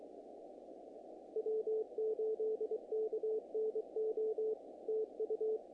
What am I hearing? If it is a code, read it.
Morse code, heard as a keyed tone, about 23 wpm, 440 Hz. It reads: W8KNO TU
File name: contests / naqp-cw-k5tr-op-k5pi- -1900